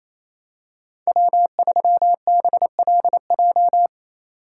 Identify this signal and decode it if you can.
Morse code, heard as a keyed tone, about 28 wpm, 700 Hz. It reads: W3BLJ
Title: USA Call Signs - 5 Characters - 2x Character Spacing - 28wpm